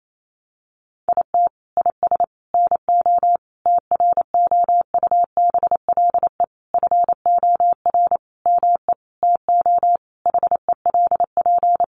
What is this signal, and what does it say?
Morse code, heard as a keyed tone, about 28 wpm, 700 Hz. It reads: IT IS NO TROUBLE FOR ME TO HELP
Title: Sentences from Top 1200 Words 28wpm